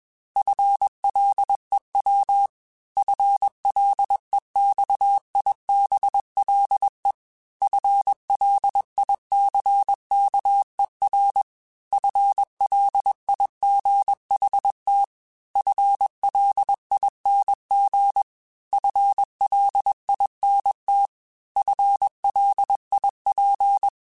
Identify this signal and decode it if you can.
Morse code, heard as a keyed tone, about 21 wpm, 800 Hz. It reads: FLEW FLEXIBLE FLICKER FLIGHT FLING FLINT FLIP
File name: FL Runs / FL Runs mp3s / FL Runs @20